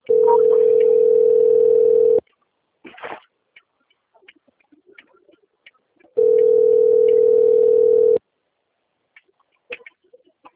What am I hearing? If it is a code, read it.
Ringback tone (North America), 440 + 480 Hz.
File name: out- - - - - .19594